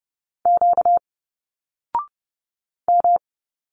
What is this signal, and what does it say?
Morse code, heard as a keyed tone, about 30 wpm, 700 Hz. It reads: Q M
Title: Single Letters 30wpm